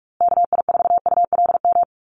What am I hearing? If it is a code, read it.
Morse code, heard as a keyed tone, about 44 wpm, 700 Hz. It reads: KI4ULM
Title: USA Call Signs Repeated 3x 45wpm